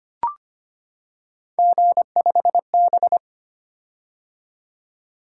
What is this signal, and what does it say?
Morse code, heard as a keyed tone, about 25 wpm, 700 Hz. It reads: G5B
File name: Three Letter-Number Combinations - 25wpm